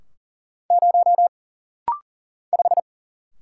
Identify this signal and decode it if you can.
Morse code, heard as a keyed tone, about 40 wpm, 700 Hz. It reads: 0 5
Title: Single Letter-Number Alt - Rapid-Fire 40wpm